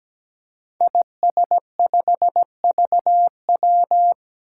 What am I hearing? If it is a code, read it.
Morse code, heard as a keyed tone, about 17 wpm, 700 Hz. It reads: IS5VW
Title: Five Challenging Letter-Number Combinations 17wpm